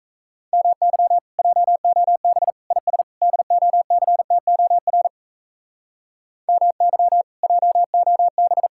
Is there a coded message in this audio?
Yes: Morse code from 0.5 to 8.7 s, heard as a keyed tone, about 37 wpm, 700 Hz. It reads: MY JOB IS DOCTOR MY JOB